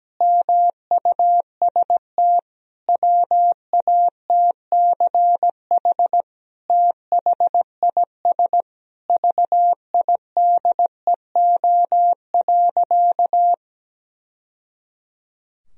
Morse code, heard as a keyed tone, about 17 wpm, 700 Hz. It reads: MUST WATCH THIS VIDEO.
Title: Sentences from Top 1000 Words - Review Part 4 - 17wpm